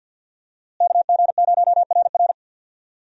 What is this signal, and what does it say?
Morse code, heard as a keyed tone, about 48 wpm, 700 Hz. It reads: KG0RR
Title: USA Call Signs - 5 Characters - Repeated 3x - 50wpm